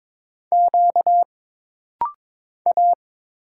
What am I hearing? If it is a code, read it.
Morse code, heard as a keyed tone, about 22 wpm, 700 Hz. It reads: Q A
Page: This rapid-fire timing is geared to push your ICR (Instant Character Recognition) skills.